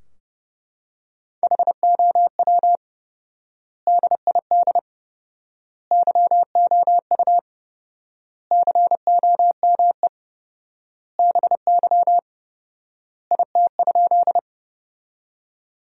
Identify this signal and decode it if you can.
Morse code, heard as a keyed tone, about 30 wpm, 700 Hz. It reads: HOW DID YOU COME BY IT?
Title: 4x Word-Spacing - Sentences from Top 100 Words - 30wpm